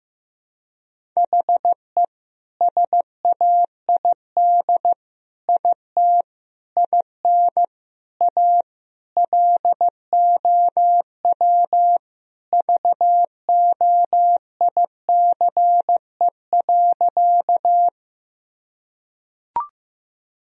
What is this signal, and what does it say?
Morse code, heard as a keyed tone, about 16 wpm, 700 Hz. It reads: HE SAID IT IN A LOW VOICE.
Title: Sentences from Top 1000 Words - Review Part 1 - 15wpm